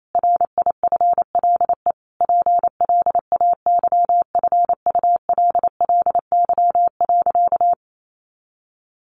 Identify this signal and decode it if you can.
Morse code, heard as a keyed tone, about 28 wpm, 700 Hz. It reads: RIFLE PLAYFULLY.
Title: A Sound Of Thunder 28wpm